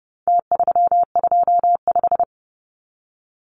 Morse code, heard as a keyed tone, about 30 wpm, 700 Hz. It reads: T325
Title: Numbers - Contest Style - 4-Digit 30wpm